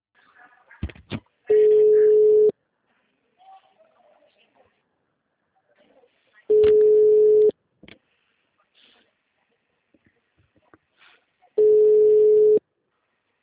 Ringback tone (Europe, 425 Hz).